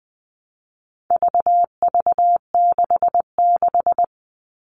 Morse code, heard as a keyed tone, about 20 wpm, 700 Hz. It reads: VV66
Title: All Cut Numbers - 4-Digit 20wpm